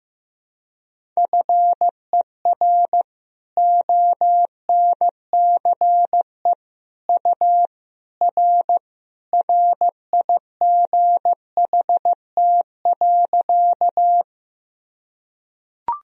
Morse code, heard as a keyed tone, about 15 wpm, 700 Hz. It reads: FER ONCE U R RIGHT.
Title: Sentences from Top 300 Words and Abbreviations 15wpm